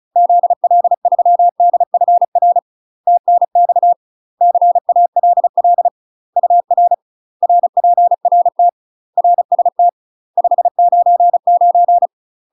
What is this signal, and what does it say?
Morse code, heard as a keyed tone, about 34 wpm, 700 Hz. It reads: ZL3DFR TNX CALL UR RPRT RST 599
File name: UK MASTER QSOs TRAINING / 35WPM / UK MASTER QSOs 0017